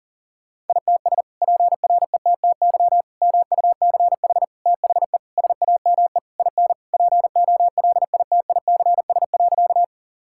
Morse code, heard as a keyed tone, about 40 wpm, 700 Hz. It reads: ITS PRETTY MUCH THE SAME IN POLITICS.
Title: Sentences from Top 1000 Words - Review Part 6 - 40wpm